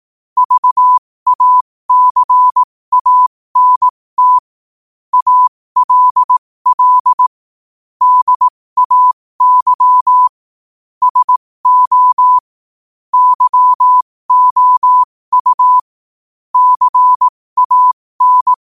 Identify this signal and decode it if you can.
Morse code, heard as a keyed tone, about 18 wpm, spaced as for 15 wpm, 1000 Hz. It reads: VACANT ALL DAY SO YOU CAN
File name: qotdfiles / 15wpm